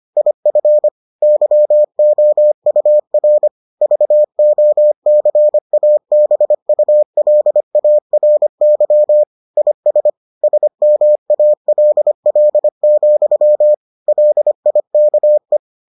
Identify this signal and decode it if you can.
Morse code, heard as a keyed tone, about 25 wpm, 600 Hz. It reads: IF YOUR VOCABULARY IS SMALL, LIKE